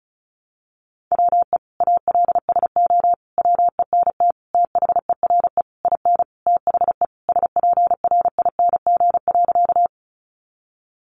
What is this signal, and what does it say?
Morse code, heard as a keyed tone, about 34 wpm, 700 Hz. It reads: WE ALSO WENT THERE IN THE SPRING.
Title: Sentences from Top 1200 Words 35wpm